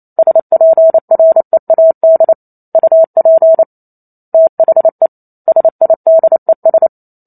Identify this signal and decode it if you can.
Morse code, heard as a keyed tone, about 28 wpm, 650 Hz. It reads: SPREAD UP THE SIDES